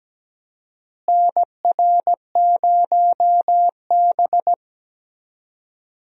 Morse code, heard as a keyed tone, about 17 wpm, 700 Hz. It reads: NR0B
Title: USA Call Signs - 4 Characters - Repeated 3x - 17wpm